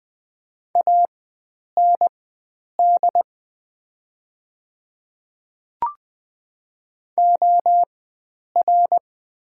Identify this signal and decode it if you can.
Morse code, heard as a keyed tone, about 20 wpm, spaced as for 9 wpm, 700 Hz. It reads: AND OR
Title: 4x Character Spacing - Top 100 Words - 20wpm